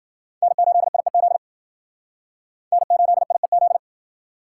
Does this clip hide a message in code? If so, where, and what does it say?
Yes: Morse code from 0.4 to 3.8 s, heard as a keyed tone, about 52 wpm, 700 Hz. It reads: N8IEZ N8IEZ